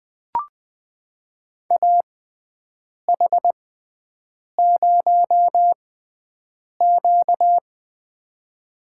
Morse code, heard as a keyed tone, about 20 wpm, spaced as for 7 wpm, 700 Hz. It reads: AH0Q
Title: USA Call Signs - 4 Characters - 6x Character Spacing - 20wpm